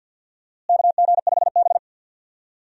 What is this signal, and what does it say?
Morse code, heard as a keyed tone, about 52 wpm, 700 Hz. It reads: KG5B
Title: USA Call Signs - 4 Characters - Repeated 2x - 50wpm